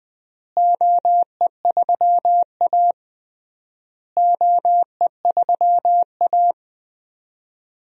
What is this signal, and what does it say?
Morse code, heard as a keyed tone, about 20 wpm, 700 Hz. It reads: OE3A OE3A
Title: DXCC-100 Call Signs - 4 Characters - Repeated 3x - 20wpm